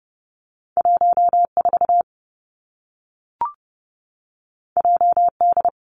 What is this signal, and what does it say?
Morse code, heard as a keyed tone, about 30 wpm, 700 Hz. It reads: 1 4 J D
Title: Two Challenging Letter-Number Combinations 30wpm